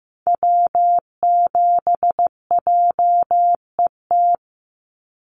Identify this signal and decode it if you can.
Morse code, heard as a keyed tone, about 15 wpm, 700 Hz. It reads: W7JET
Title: NA SOTA Activator Callsigns 15wpm